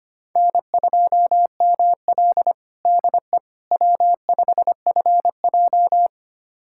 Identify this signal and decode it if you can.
Morse code, heard as a keyed tone, about 25 wpm, 700 Hz. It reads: N2ML DE W5FJ
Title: USA Callsign DE Callsign - 4 Characters - 25wpm